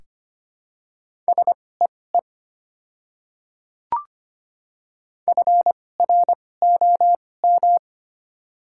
Morse code, heard as a keyed tone, about 25 wpm, spaced as for 18 wpm, 700 Hz. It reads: SEE FROM
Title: 2x Character Spacing - Top 100 Words - 25wpm